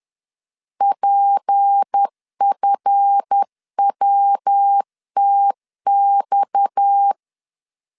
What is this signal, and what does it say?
Morse code, heard as a keyed tone, about 10 wpm, 800 Hz. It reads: PFWTX
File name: LEZIONI / 8 YJ45 / 8 YJ45 10wpm 2549